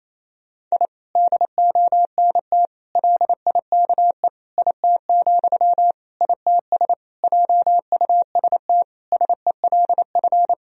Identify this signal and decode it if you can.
Morse code, heard as a keyed tone, about 28 wpm, 700 Hz. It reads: I DONT LIKE IT, ITS JUST SELF
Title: Sentences from Top 1000 Words - Review Part 1 - 28wpm